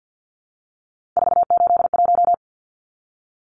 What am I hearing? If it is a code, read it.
Morse code, heard as a keyed tone, about 50 wpm, 700 Hz. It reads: 481
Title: Numbers - 3-Digit 50wpm